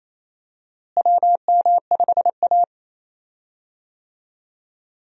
Morse code, heard as a keyed tone, about 27 wpm, 700 Hz. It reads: WM5A
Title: USA Call Signs - 4 Characters - Repeated 3x - 28wpm